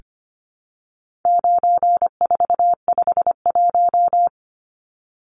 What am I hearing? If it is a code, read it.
Morse code, heard as a keyed tone, about 25 wpm, 700 Hz. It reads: 9451
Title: Numbers - 4-Digit 25wpm